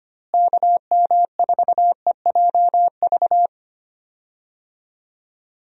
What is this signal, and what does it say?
Morse code, heard as a keyed tone, about 25 wpm, 700 Hz. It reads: KM4EJV
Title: USA Call Signs - 6 Characters - Repeated 2x - 25wpm